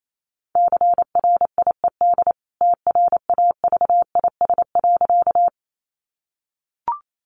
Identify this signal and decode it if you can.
Morse code, heard as a keyed tone, about 28 wpm, 700 Hz. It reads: CRIED TRAVIS.
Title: A Sound Of Thunder 28wpm